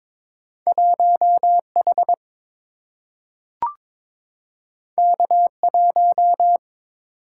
Morse code, heard as a keyed tone, about 22 wpm, 690 Hz. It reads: 1H K1
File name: practice / 22wpm / Two Letter-Number Combinations - 22wpm